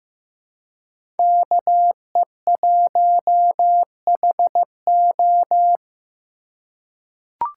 Morse code, heard as a keyed tone, about 15 wpm, 700 Hz. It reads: KE1HO
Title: USA Call Signs - 5 Characters - 3x Character Spacing - 15wpm